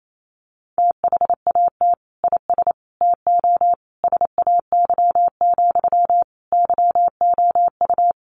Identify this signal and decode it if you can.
Morse code, heard as a keyed tone, about 28 wpm, 700 Hz. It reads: THAT IS TO SAY, YOU
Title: Sentences from Top 700 Words 28wpm